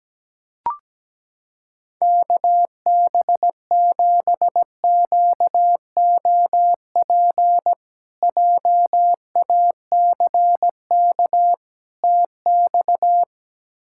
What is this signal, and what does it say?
Morse code, heard as a keyed tone, about 17 wpm, 700 Hz. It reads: KB7QOP JACK TX
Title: K1USN SST QSO Practice - 17wpm